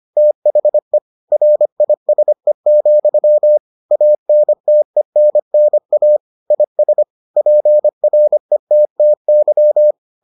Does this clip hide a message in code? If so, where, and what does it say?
Yes: Morse code from 0.2 to 9.9 s, heard as a keyed tone, about 25 wpm, 600 Hz. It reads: THE RISE, ANTENNA IS PRETTY